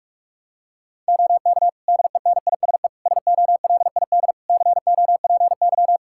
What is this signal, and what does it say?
Morse code, heard as a keyed tone, about 46 wpm, 700 Hz. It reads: OK DENISE SOLID COPY
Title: Practice QSO - Exchange 3 - More Info - 45wpm